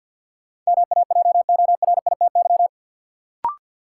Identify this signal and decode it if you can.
Morse code, heard as a keyed tone, about 53 wpm, 700 Hz. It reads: MAJORITY
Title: Sets of 3 Words - New Word Drill - 50wpm